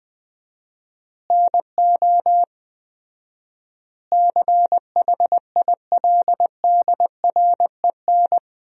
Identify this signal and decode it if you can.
Morse code, heard as a keyed tone, about 20 wpm, 700 Hz. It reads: NO CHILDREN
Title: 4x Word-Spacing - Sentences from Top 400 Words - 20wpm